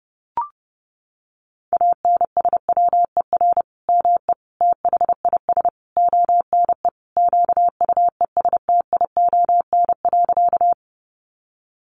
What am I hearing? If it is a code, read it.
Morse code, heard as a keyed tone, about 30 wpm, 700 Hz. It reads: ANSWER ME THIS ONE QUESTION.
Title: Sentences from Top 400 Words 30wpm